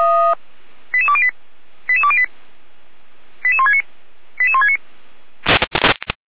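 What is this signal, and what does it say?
Five-tone selective call (ZVEI), heard as quick runs of five tones, tones of about 70 ms. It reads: 88208 88208 88169 88169
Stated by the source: Five-tone ZVEI selective call received on an SDR.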